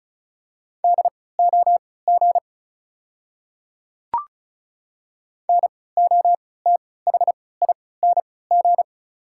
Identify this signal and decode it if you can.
Morse code, heard as a keyed tone, about 34 wpm, spaced as for 20 wpm, 690 Hz. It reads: DOG NOTHING
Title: 3x Character Spacing - Top 500 Words - 35wpm